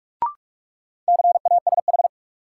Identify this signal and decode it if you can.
Morse code, heard as a keyed tone, about 41 wpm, 700 Hz. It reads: CASH
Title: Sentences from Top 700 Words - New Word Drill - 45wpm